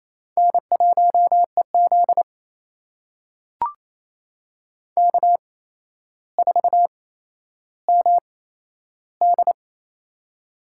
Morse code, heard as a keyed tone, about 28 wpm, spaced as for 12 wpm, 700 Hz. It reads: N1EZ K4MD
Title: USA Call Signs - 4 Characters - 8x Character Spacing - 28wpm